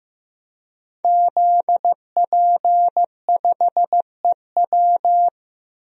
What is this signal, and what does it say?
Morse code, heard as a keyed tone, about 15 wpm, 700 Hz. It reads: ZP5EW
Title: DXCC-100 Call Signs - 5 Characters - Repeated 2x - 15wpm